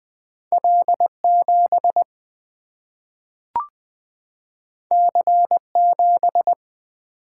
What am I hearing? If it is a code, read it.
Morse code, heard as a keyed tone, about 20 wpm, 700 Hz. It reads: L7 C7
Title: Single-Letter Single-Number 20wpm